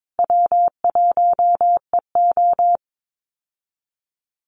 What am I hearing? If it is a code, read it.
Morse code, heard as a keyed tone, about 22 wpm, 700 Hz. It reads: W1EO
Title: USA Call Signs Repeated 3x 22wpm